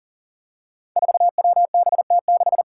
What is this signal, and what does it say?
Morse code, heard as a keyed tone, about 40 wpm, 700 Hz. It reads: 4WBT6
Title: Five Challenging Letter-Number Combinations 40wpm